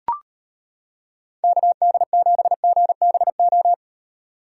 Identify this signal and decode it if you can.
Morse code, heard as a keyed tone, about 30 wpm, 700 Hz. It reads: KD7GBO
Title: Speed Racing - USA 2x3 Call Signs 25wpm